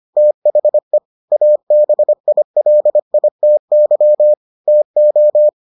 Morse code, heard as a keyed tone, about 25 wpm, 600 Hz. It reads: THE ABILITY TO